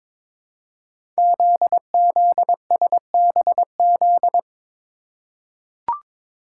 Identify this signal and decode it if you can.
Morse code, heard as a keyed tone, about 22 wpm, 700 Hz. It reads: ZZSBZ
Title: Code Groups - 5 Letters - 22wpm